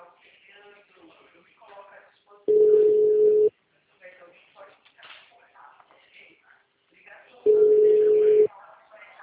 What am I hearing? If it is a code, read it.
Ringback tone (Europe, 425 Hz), with speech.
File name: out- -2001- - - .62979